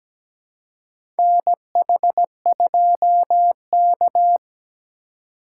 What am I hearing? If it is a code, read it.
Morse code, heard as a keyed tone, about 17 wpm, 700 Hz. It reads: NH2K
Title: USA Call Signs - 4 Characters - 2x Character Spacing - 17wpm